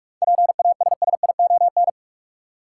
Morse code, heard as a keyed tone, about 44 wpm, 700 Hz. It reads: PASSION
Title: Sentences from Top 1300 Words - New Word Drill - 45wpm